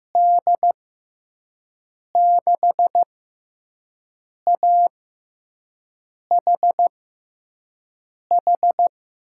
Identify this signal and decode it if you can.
Morse code, heard as a keyed tone, about 15 wpm, spaced as for 5 wpm, 700 Hz. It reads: D6AHH
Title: USA Call Signs - 6x Character Spacing - 15wpm